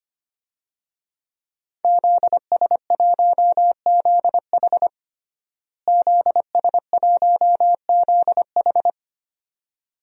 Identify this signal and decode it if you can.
Morse code, heard as a keyed tone, about 25 wpm, 700 Hz. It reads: ZS1ZH ZS1ZH
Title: DXCC-100 Call Signs - 5 Characters - Repeated 2x - 25wpm